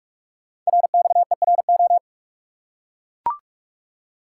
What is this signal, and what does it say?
Morse code, heard as a keyed tone, about 44 wpm, 700 Hz. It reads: RXERO
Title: Code Groups - 5 Letters - 45wpm